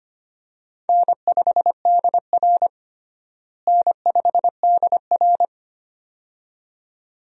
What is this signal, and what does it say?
Morse code, heard as a keyed tone, about 25 wpm, 700 Hz. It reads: N5DR N5DR
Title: USA Call Signs - 4 Characters - Repeated 2x - 25wpm